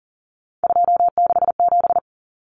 Morse code, heard as a keyed tone, about 40 wpm, 700 Hz. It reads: 267
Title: Common Cut Numbers - 3-Digit 40wpm